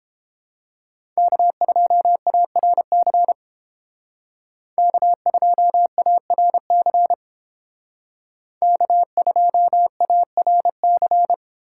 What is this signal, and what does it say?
Morse code, heard as a keyed tone, about 31 wpm, 700 Hz. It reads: K2ARC K2ARC K2ARC